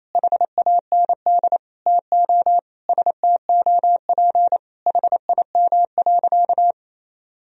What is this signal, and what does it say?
Morse code, heard as a keyed tone, about 28 wpm, 700 Hz. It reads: HAND TO STOP HIM.
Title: Sentences from Top 1000 Words - Review Part 4 - 28wpm